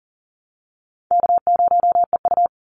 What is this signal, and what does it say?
Morse code, heard as a keyed tone, about 40 wpm, 700 Hz. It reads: K0EU
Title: Call Signs Repeated 2x - 40wpm